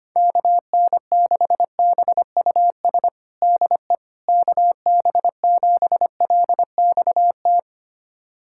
Morse code, heard as a keyed tone, about 25 wpm, 700 Hz. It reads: KN6BUS DE KB7LXT
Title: USA Callsign DE Callsign - 6 Characters - 25wpm